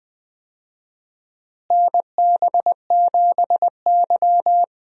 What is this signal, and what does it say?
Morse code, heard as a keyed tone, about 21 wpm, 700 Hz. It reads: NB7Y
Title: USA Call Signs - 4 Characters - Repeated 2x - 20wpm